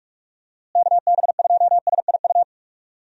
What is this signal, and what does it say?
Morse code, heard as a keyed tone, about 41 wpm, 700 Hz. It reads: KB2SIU
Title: USA Call Signs - 6 Characters - Repeated 2x - 45wpm